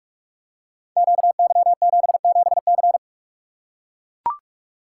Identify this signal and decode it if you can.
Morse code, heard as a keyed tone, about 44 wpm, 700 Hz. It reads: QY77C
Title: Code Groups - 5 Letter:Numbers - 45wpm